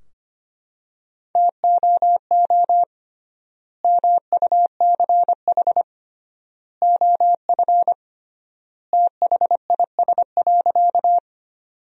Morse code, heard as a keyed tone, about 25 wpm, 700 Hz. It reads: TOO MUCH OF THIS.
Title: 3x Word-Spacing - Sentences from Top 200 Words - 25wpm